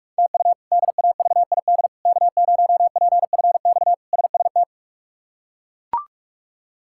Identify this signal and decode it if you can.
Morse code, heard as a keyed tone, about 41 wpm, 700 Hz. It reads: TU DAVID K0PFX SST
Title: K1USN SST QSO Practice - 45wpm